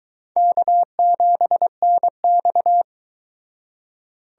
Morse code, heard as a keyed tone, about 22 wpm, 700 Hz. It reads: K7NX